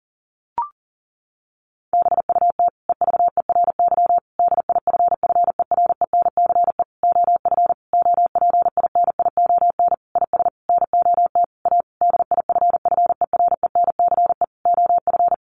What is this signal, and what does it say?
Morse code, heard as a keyed tone, about 40 wpm, 700 Hz. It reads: BUT EVERY DIFFERENCE OF OPINION IS NOT A DIFFERENCE OF
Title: Sentences from Top 1400 Words 40wpm